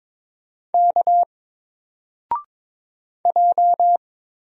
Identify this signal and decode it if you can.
Morse code, heard as a keyed tone, about 22 wpm, 700 Hz. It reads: K J
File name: Single-Letter-Number-22wpm